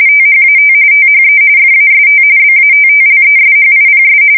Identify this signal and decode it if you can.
RTTY, about 45.45 baud. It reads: THAT THE FAST BATTLESHIPS
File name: fsk170 / battleshipnj